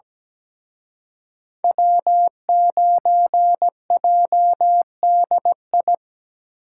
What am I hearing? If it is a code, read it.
Morse code, heard as a keyed tone, about 20 wpm, 700 Hz. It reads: W9JDI
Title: Speed Racing - USA 1x3 Call Signs 17wpm